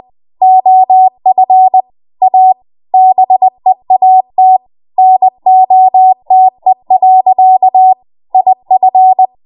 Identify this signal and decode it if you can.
Morse code, heard as a keyed tone, about 20 wpm, 750 Hz. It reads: OF A BEAT NOTE. IF
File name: Morse / 20WPM